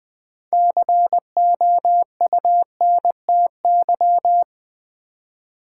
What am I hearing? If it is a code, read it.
Morse code, heard as a keyed tone, about 20 wpm, 700 Hz. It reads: COUNTY
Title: Six Letter Words - 20wpm